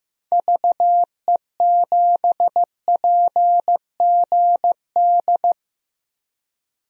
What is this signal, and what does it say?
Morse code, heard as a keyed tone, about 15 wpm, 700 Hz. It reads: VE7PGD
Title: DXCC-100 Call Signs - 6 Characters - 15wpm